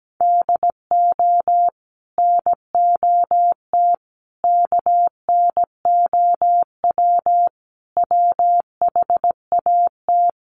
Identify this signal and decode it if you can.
Morse code, heard as a keyed tone, about 17 wpm, 700 Hz. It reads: DO NOT KNOW WHAT
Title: Sets of 5 Words 17wpm